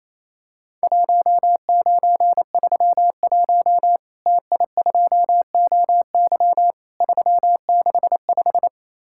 Morse code, heard as a keyed tone, about 28 wpm, 700 Hz. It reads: 1931 TI2OY 365
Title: Worked All Europe DX Contest - QTC Exchange - 28wpm